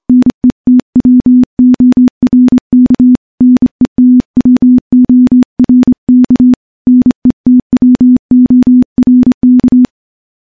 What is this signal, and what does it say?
Morse code, heard as a keyed tone, about 26 wpm, 250 Hz. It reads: NETWORK NETWORK NETWORK
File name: morse-training-25WPM